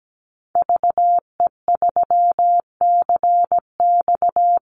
Morse code, heard as a keyed tone, about 17 wpm, 700 Hz. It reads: VE3CX
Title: Call Signs - 17wpm